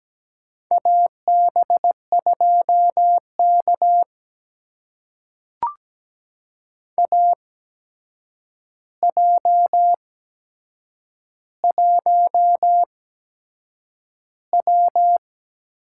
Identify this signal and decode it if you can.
Morse code, heard as a keyed tone, about 17 wpm, spaced as for 7 wpm, 700 Hz. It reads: AB2K AJ1W
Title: USA Call Signs - 4 Characters - 8x Character Spacing - 17wpm